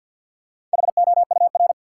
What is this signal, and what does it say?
Morse code, heard as a keyed tone, about 53 wpm, 700 Hz. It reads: HOUR
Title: Sets of 5 Words - New Word Drill - 50wpm